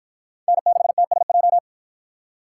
Morse code, heard as a keyed tone, about 48 wpm, 700 Hz. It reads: N6TSJ